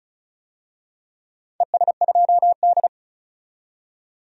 Morse code, heard as a keyed tone, about 36 wpm, 700 Hz. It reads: ES2D
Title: DXCC-100 Call Signs - 4 Characters - 35wpm